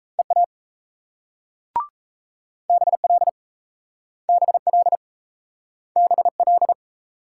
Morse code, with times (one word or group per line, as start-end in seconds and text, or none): 0.2-0.5 EA
2.7-3.3 BL
4.3-5.0 BL
6.0-6.7 BL